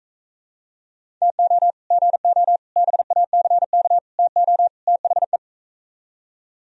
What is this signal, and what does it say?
Morse code, heard as a keyed tone, about 42 wpm, 700 Hz. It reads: TO GO BACK TO THE